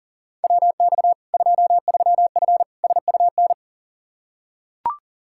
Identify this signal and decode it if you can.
Morse code, heard as a keyed tone, about 40 wpm, 700 Hz. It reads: WX 23F SUN
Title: Practice QSO - Exchange 2 - Info - 40wpm